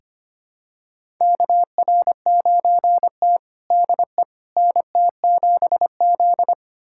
Morse code, heard as a keyed tone, about 25 wpm, 700 Hz. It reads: KR9T DE NT7Z
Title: USA Callsign DE Callsign - 4 Characters - 25wpm